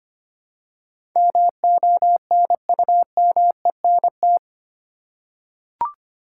Morse code, heard as a keyed tone, about 25 wpm, 700 Hz. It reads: MONUMENT
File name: 8 Letter Words - First 3 Letters Given - 25wpm